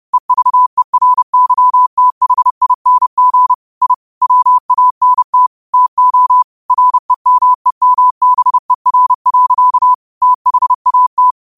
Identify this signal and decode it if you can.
Morse code, heard as a keyed tone, about 30 wpm, 1000 Hz. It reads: EVERYTHING I WANT TO REMEMBER. THAT
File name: qotdfiles / 30wpm